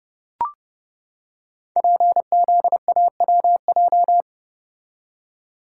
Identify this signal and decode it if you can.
Morse code, heard as a keyed tone, about 30 wpm, 700 Hz. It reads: PZAWJ
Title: Code Groups - 5 Letter:Numbers - 30wpm